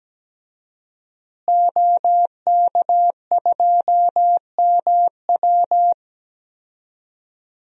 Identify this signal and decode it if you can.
Morse code, heard as a keyed tone, about 17 wpm, 700 Hz. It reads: OK2MW
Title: DXCC-100 Call Signs - 5 Characters - Repeated 3x - 17wpm